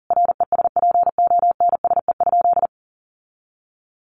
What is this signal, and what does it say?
Morse code, heard as a keyed tone, about 40 wpm, 700 Hz. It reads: RESPONSE?
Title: Sentences from Top 1400 Words 40wpm